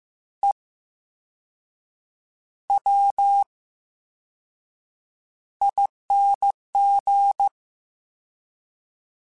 Morse code, heard as a keyed tone, about 15 wpm, spaced as for 6 wpm, 800 Hz. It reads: EWING
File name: Special INGs-mp3s / Special INGs-mp3s / VerySlowToFaster Ing @13